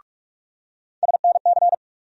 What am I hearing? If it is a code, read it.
Morse code, heard as a keyed tone, about 45 wpm, 700 Hz. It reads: SNC
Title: USA Call Sign Postfixes - 3 Letters 45wpm